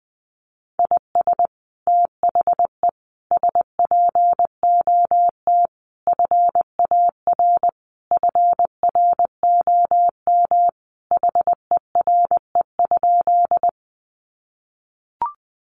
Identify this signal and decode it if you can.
Morse code, heard as a keyed tone, about 20 wpm, 700 Hz. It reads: IS THE SPOT FAR FROM HERE?
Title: Sentences from Top 1100 Words 20wpm